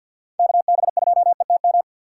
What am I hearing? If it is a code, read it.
Morse code, heard as a keyed tone, about 52 wpm, 700 Hz. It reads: KB2ETM
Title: USA Call Signs - 6 Characters - 3x Character Spacing - 50wpm